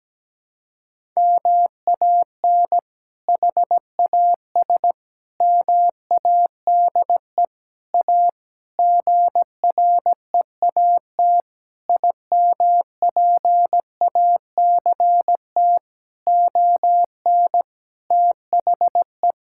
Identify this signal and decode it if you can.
Morse code, heard as a keyed tone, about 17 wpm, 700 Hz. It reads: MAN HAS MADE A GREAT IMPACT ON THE
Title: Sentences from Top 1000 Words - Review Part 6 - 17wpm